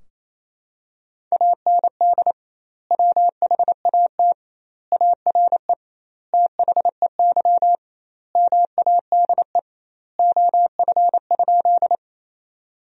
Morse code, heard as a keyed tone, about 28 wpm, 700 Hz. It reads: AND WHAT ARE THEY MADE OF?
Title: 2x Word-Spacing - Sentences from Top 200 Words - 28wpm